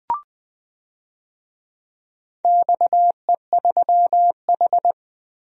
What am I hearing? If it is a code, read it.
Morse code, heard as a keyed tone, about 19 wpm, 700 Hz. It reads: XE3H
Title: DXCC-100 Call Signs - 4 Characters - Repeated 2x - 20wpm